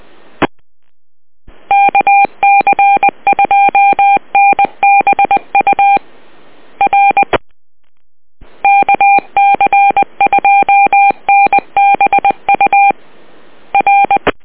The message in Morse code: E KC2NBU R KC2NBU R